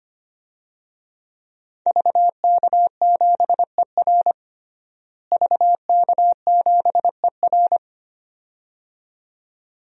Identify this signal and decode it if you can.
Morse code, heard as a keyed tone, about 25 wpm, 700 Hz. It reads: VK7ER VK7ER
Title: DXCC-100 Call Signs - 5 Characters - Repeated 2x - 25wpm